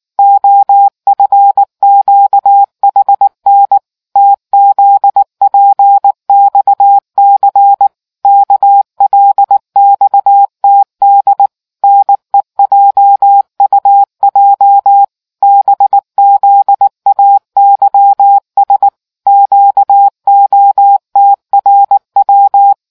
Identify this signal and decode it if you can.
Morse code, heard as a keyed tone, about 19 wpm, 800 Hz. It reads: OFQHN TZPXC KLXTD NEJUJ BZAYS QOTRW